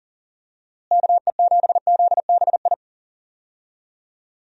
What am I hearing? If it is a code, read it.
Morse code, heard as a keyed tone, about 40 wpm, 700 Hz. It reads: KE7ZBI
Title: USA Call Signs - 6 Characters - Repeated 2x - 40wpm